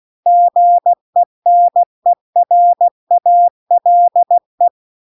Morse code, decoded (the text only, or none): GENERALE